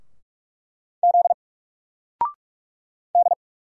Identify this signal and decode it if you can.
Morse code, heard as a keyed tone, about 46 wpm, 700 Hz. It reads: Z D
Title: Single Letter-Number Alt 45wpm